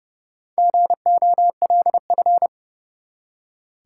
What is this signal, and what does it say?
Morse code, heard as a keyed tone, about 32 wpm, 700 Hz. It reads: GOLF
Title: Four Letter Words - 30wpm